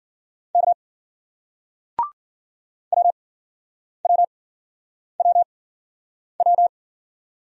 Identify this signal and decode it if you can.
Morse code, heard as a keyed tone, about 50 wpm, spaced as for 9 wpm, 700 Hz. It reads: K WWWW
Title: Speed-Racing Single Letters 40wpm